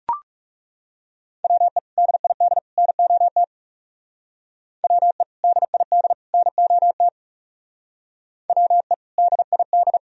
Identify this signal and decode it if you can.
Morse code, heard as a keyed tone, about 36 wpm, 700 Hz. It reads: WE DID NOT WE DID NOT WE DID
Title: Speed-Racing - Sets of 3 Words 30wpm